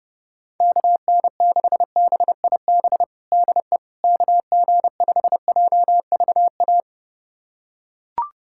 Morse code, heard as a keyed tone, about 30 wpm, 700 Hz. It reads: KN6BIB DE KG5JVA
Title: USA Callsign DE Callsign - 6 Characters - 30wpm